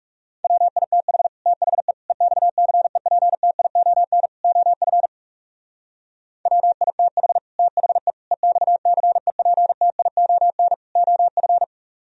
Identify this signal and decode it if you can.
Morse code, heard as a keyed tone, about 37 wpm, 700 Hz. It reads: WITH THE EXCEPTION OF WITH THE EXCEPTION OF
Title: Speed-Racing - Sets of 4 Words 30wpm